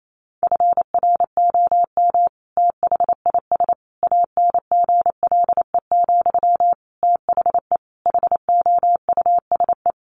Morse code, heard as a keyed tone, about 28 wpm, 700 Hz. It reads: FROM THIS ANGLE, THE HOUSE
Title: Sentences from Top 1500 Words 28wpm